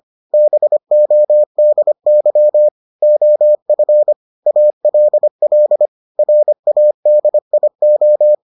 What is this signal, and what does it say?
Morse code, heard as a keyed tone, about 25 wpm, 600 Hz. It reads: BODY OF ALL RADIO